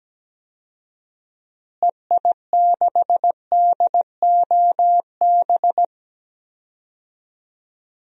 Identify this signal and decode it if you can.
Morse code, heard as a keyed tone, about 17 wpm, 700 Hz. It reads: EI6DOB
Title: DXCC-100 Call Signs - 6 Characters - Repeated 3x - 17wpm